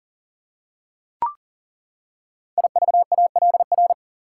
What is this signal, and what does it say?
Morse code, heard as a keyed tone, about 38 wpm, 700 Hz. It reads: IVALR
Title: Code Groups - 5 Letters - 40wpm